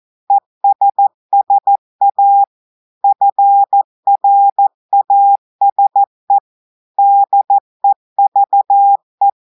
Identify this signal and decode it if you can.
Morse code, heard as a keyed tone, about 14 wpm, 800 Hz. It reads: ESSA FRASE DEVE
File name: audios / portugais / portugais texte en clair 2 vitesse 14 mots minutes